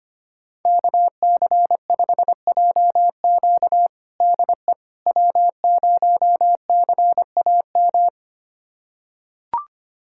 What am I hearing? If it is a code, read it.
Morse code, heard as a keyed tone, about 25 wpm, 700 Hz. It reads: KC5JQ DE W0CAM
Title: USA Callsign DE Callsign - 5 Characters - 25wpm